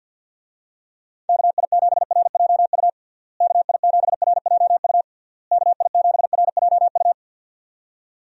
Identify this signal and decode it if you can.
Morse code, heard as a keyed tone, about 52 wpm, 700 Hz. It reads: KI7RJU KI7RJU KI7RJU
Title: USA Call Signs - 6 Characters - Repeated 3x - 50wpm